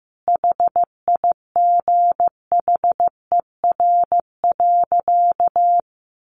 Morse code, heard as a keyed tone, about 15 wpm, 700 Hz. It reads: HIGHER.
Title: Additional Easy Phrases 15wpm